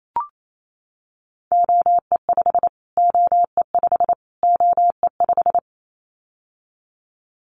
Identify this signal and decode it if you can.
Morse code, heard as a keyed tone, about 28 wpm, 700 Hz. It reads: OE5 OE5 OE5
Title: Callsign Prefixes for Top 25 DXCC - Repeated 3x - 28wpm